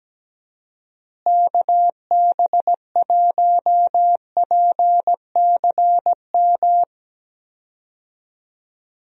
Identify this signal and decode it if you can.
Morse code, heard as a keyed tone, about 17 wpm, 700 Hz. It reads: KB1PCM
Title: USA Call Signs - 6 Characters - Repeated 3x - 17wpm